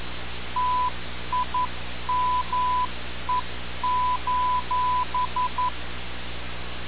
Morse code, heard as a keyed tone, about 11 wpm, 1000 Hz. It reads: TIME:
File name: radiocap / REC z